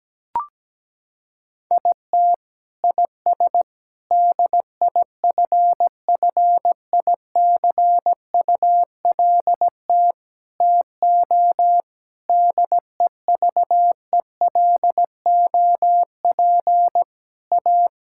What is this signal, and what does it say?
Morse code, heard as a keyed tone, about 17 wpm, 700 Hz. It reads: IT IS DIFFICULT TO DEVELOP A
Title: Sentences from Top 1000 Words - Review Part 5 - 17wpm